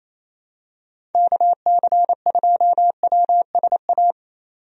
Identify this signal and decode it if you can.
Morse code, heard as a keyed tone, about 28 wpm, 700 Hz. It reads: KC2WSA
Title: USA Call Signs - 6 Characters - 28wpm